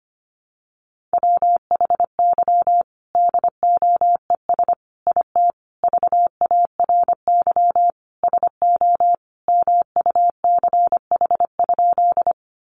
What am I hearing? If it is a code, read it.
Morse code, heard as a keyed tone, about 25 wpm, 700 Hz. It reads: WHY DOES IT VARY SO MUCH?
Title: Sentences from Top 1200 Words 25wpm